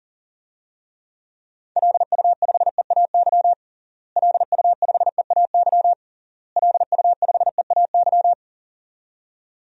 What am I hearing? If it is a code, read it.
Morse code, heard as a keyed tone, about 40 wpm, 700 Hz. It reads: LU5EAY LU5EAY LU5EAY
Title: DXCC-100 Call Signs - 6 Characters - Repeated 3x - 40wpm